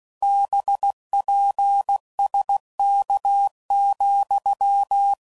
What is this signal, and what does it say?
Morse code, heard as a keyed tone, about 16 wpm, 800 Hz. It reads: BPSK,
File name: CW Code Groups @15WPM / QSOs / kc7haa